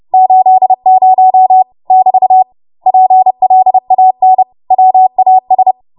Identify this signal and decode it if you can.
Morse code, heard as a keyed tone, about 30 wpm, 750 Hz. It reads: 80 <BT> PLAN WAS
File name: Morse / 30WPM